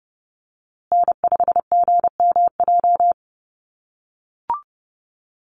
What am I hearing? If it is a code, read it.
Morse code, heard as a keyed tone, about 30 wpm, 700 Hz. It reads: N5GMJ
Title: USA Call Signs Repeated 2x 30wpm